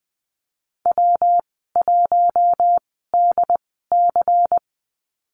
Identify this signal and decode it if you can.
Morse code, heard as a keyed tone, about 20 wpm, spaced as for 14 wpm, 700 Hz. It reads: W1DC
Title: 147 Lesson 24g - Letter y - Call Signs 20wpm